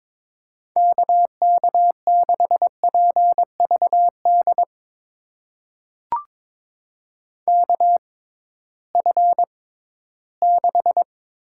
Morse code, heard as a keyed tone, about 21 wpm, spaced as for 14 wpm, 700 Hz. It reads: KK6PVD KF6
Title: USA Call Signs - 6 Characters - 6x Character Spacing - 22wpm